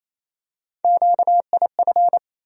Morse code, heard as a keyed tone, about 28 wpm, 700 Hz. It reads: QIF
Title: USA Call Sign Postfixes - 3 Letters 28wpm